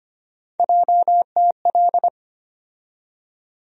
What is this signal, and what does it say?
Morse code, heard as a keyed tone, about 25 wpm, 700 Hz. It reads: JTL
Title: USA Call Sign Postfixes - 3 Letters 25wpm